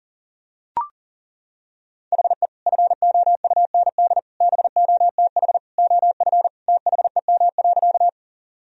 Morse code, heard as a keyed tone, about 37 wpm, 700 Hz. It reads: HE FOUND BOTH OF THEM.
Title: Sentences from Top 1000 Words - Review Part 1 - 40wpm